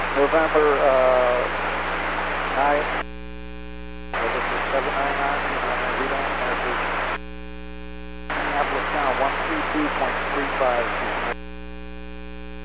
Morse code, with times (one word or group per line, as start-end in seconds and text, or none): none